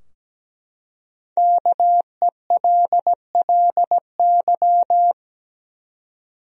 Morse code, heard as a keyed tone, about 17 wpm, 700 Hz. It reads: KELLY
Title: USA Common Names - Max 5 Letters - 17wpm